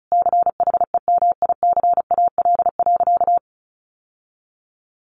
Morse code, heard as a keyed tone, about 34 wpm, 700 Hz. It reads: CHEMICAL.
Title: Sentences from Top 1200 Words 35wpm